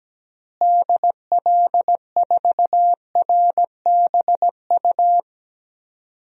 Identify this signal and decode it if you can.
Morse code, heard as a keyed tone, about 17 wpm, 700 Hz. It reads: DL4RBU
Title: DXCC-100 Call Signs - 6 Characters - Repeated 2x - 17wpm